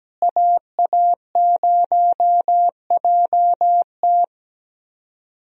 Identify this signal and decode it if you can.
Morse code, heard as a keyed tone, about 17 wpm, 700 Hz. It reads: AA0JT
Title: USA Call Signs - 5 Characters - Repeated 2x - 17wpm